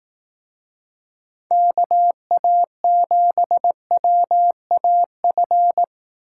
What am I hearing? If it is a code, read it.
Morse code, heard as a keyed tone, about 18 wpm, 700 Hz. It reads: KA7WAF
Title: Speed Racing - USA 2x3 Call Signs 15wpm